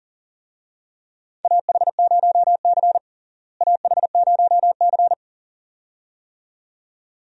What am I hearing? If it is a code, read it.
Morse code, heard as a keyed tone, about 37 wpm, 700 Hz. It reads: AH0C AH0C
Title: USA Call Signs - 4 Characters - Repeated 2x - 40wpm